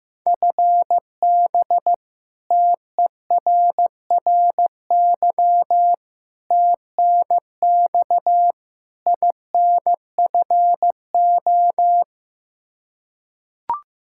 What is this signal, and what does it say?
Morse code, heard as a keyed tone, about 16 wpm, 700 Hz. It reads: FB TERRY TNX INFO
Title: Practice QSO - Exchange 3 - More Info - 15wpm